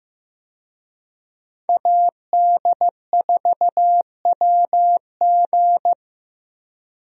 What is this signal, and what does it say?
Morse code, heard as a keyed tone, about 15 wpm, 700 Hz. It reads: AD4WG
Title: USA Call Signs - 5 Characters - Repeated 2x - 15wpm